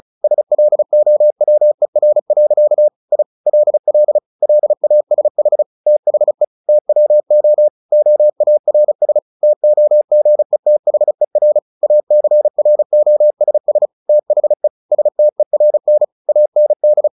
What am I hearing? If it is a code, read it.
Morse code, heard as a keyed tone, about 34 wpm, 600 Hz. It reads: SLOWER. I LL LASH THE TWO OARS TOGETHER ACROSS THE STERN AND
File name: the old man and the sea 2 35wpm